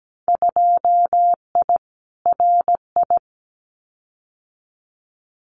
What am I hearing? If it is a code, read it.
Morse code, heard as a keyed tone, about 17 wpm, 700 Hz. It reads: 2I RI
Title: Winter Field Day Exchanges Encore 17wpm